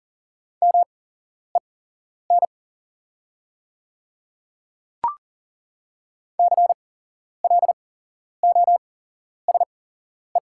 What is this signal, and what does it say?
Morse code, heard as a keyed tone, about 37 wpm, spaced as for 11 wpm, 690 Hz. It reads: MEN CLOSE
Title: 8x Character Spacing - Top 300 Words - 40wpm